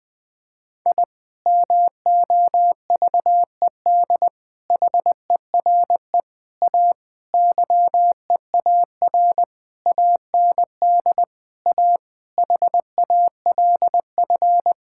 Morse code, heard as a keyed tone, about 21 wpm, 700 Hz. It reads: I MOVED HERE A YEAR AND A HALF
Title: Sentences with 6-Word Phrases - 20wpm